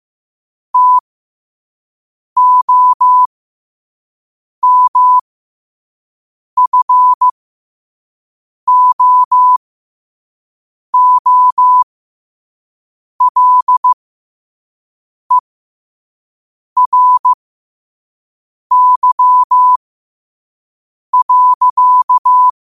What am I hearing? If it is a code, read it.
Morse code, heard as a keyed tone, about 15 wpm, spaced as for 5 wpm, 1000 Hz. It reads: TOMFOOLERY.
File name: qotdfiles / 5wpm